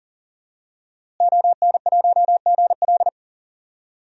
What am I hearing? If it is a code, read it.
Morse code, heard as a keyed tone, about 38 wpm, 700 Hz. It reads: ON1GL
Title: DXCC-100 Call Signs - 5 Characters - Repeated 2x - 40wpm